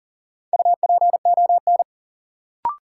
Morse code, heard as a keyed tone, about 40 wpm, 700 Hz. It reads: UPON
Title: Sentences from Top 1100 Words - New Word Drill Alt - 40wpm